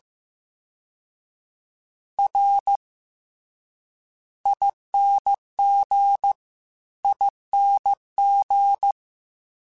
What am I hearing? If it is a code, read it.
Morse code, heard as a keyed tone, about 15 wpm, spaced as for 10 wpm, 800 Hz. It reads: RINGING
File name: Special INGs-mp3s / Special INGs-mp3s / VerySlowToFaster Ing @13